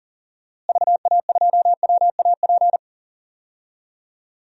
Morse code, heard as a keyed tone, about 40 wpm, 700 Hz. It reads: VA2WAP
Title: DXCC-100 Call Signs - 6 Characters - Repeated 2x - 40wpm